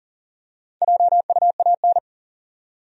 Morse code, heard as a keyed tone, about 41 wpm, 700 Hz. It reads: JUAN
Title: USA Common Names - Max 4 Letters - 40wpm